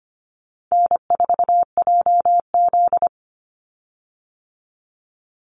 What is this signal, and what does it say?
Morse code, heard as a keyed tone, about 25 wpm, 700 Hz. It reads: N4JZ
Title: USA Call Signs 25wpm